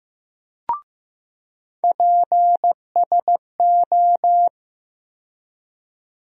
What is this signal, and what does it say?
Morse code, heard as a keyed tone, about 16 wpm, 700 Hz. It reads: PSO
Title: USA Call Sign Postfixes - 3 Letters 15wpm